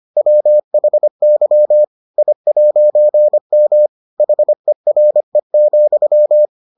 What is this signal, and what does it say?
Morse code, heard as a keyed tone, about 25 wpm, 600 Hz. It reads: WHY I'M HERE,